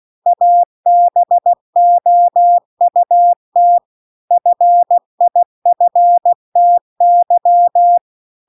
Morse code, heard as a keyed tone, about 16 wpm, 700 Hz. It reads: ABOUT FIFTY